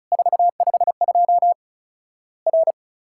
Morse code, heard as a keyed tone, about 34 wpm, 700 Hz. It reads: 452 R
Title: Worked All Europe DX Contest - QTC Exchange - 35wpm